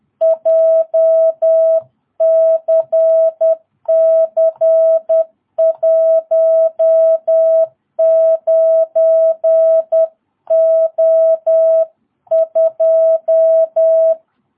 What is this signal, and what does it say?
Morse code, heard as a keyed tone, about 10 wpm, 660 Hz. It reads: JCC19O2
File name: jh8cbh / 2022cw